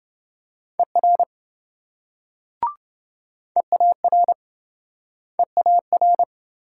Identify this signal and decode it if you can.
Morse code, heard as a keyed tone, about 28 wpm, 700 Hz. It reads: ER EAR EAR